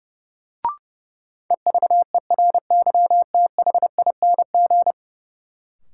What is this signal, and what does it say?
Morse code, heard as a keyed tone, about 29 wpm, 700 Hz. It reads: EVERYTHING
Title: Sentences from Top 700 Words - New Word Drill - 30wpm